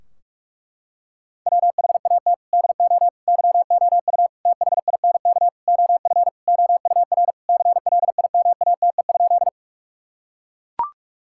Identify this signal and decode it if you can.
Morse code, heard as a keyed tone, about 43 wpm, 700 Hz. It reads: WHAT DO YOU THINK OF OUR CLIMATE?
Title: Sentences from Top 1600 Words 45wpm